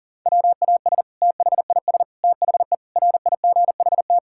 Morse code, heard as a keyed tone, about 40 wpm, 700 Hz. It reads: WAS THIS THE RIGHT
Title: Sentences from Top 1600 Words 40wpm